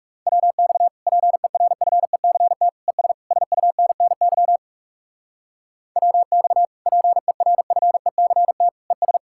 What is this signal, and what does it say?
Morse code, heard as a keyed tone, about 36 wpm, 700 Hz. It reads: WX PERFECT ES SUNNY WX PERFECT ES
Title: Speed-Racing Common QSO Phrases Encore - 30wpm